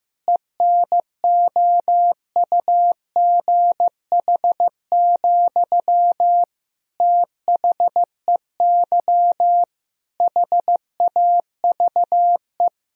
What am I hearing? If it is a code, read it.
Morse code, heard as a keyed tone, about 15 wpm, 700 Hz. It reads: ENOUGH, THEY HAVE
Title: Sentences from Top 1000 Words - Review Part 2 - 15wpm